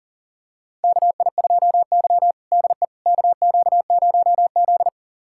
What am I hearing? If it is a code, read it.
Morse code, heard as a keyed tone, about 40 wpm, 700 Hz. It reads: KI2Y DE KQ0Z
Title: USA Callsign DE Callsign - 4 Characters - 40wpm